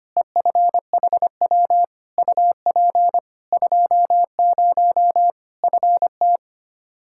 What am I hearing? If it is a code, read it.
Morse code, heard as a keyed tone, about 25 wpm, 700 Hz. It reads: EFHW UP 20 FT
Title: K1USN SST QSO Practice 25wpm